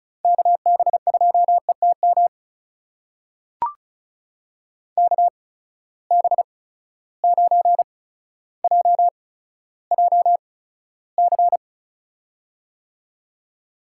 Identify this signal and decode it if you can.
Morse code, heard as a keyed tone, about 36 wpm, spaced as for 15 wpm, 700 Hz. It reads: KB2ETM KB9JJC
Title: USA Call Signs - 6 Characters - 8x Character Spacing - 35wpm